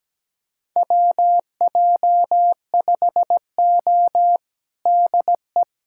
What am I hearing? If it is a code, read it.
Morse code, heard as a keyed tone, about 17 wpm, 700 Hz. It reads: WJ5O DE
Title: USA Callsign DE Callsign - 4 Characters - 17wpm